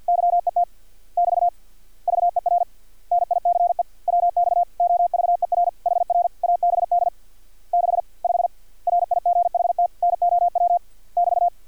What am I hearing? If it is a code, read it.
Morse code, heard as a keyed tone, about 50 wpm, 700 Hz. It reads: YET <BT> VER NICE WX OVER HR ABD 6 5 RIGHT NOW <BT>
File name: archived / CRSnet